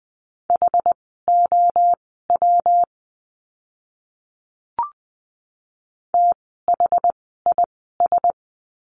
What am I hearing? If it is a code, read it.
Morse code, heard as a keyed tone, about 20 wpm, spaced as for 14 wpm, 700 Hz. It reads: HOW THIS
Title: 139 Lesson 23f - Letter f - Words 20wpm